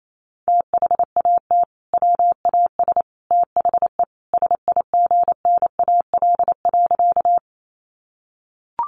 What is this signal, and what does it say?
Morse code, heard as a keyed tone, about 28 wpm, 700 Hz. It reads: THAT WAS THE SIGNAL.
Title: Sentences from Top 1400 Words 28wpm